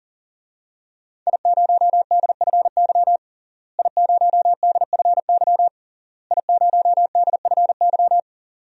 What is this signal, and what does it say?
Morse code, heard as a keyed tone, about 40 wpm, 700 Hz. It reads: I0DFY I0DFY I0DFY
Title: DXCC-100 Call Signs - 5 Characters - Repeated 3x - 40wpm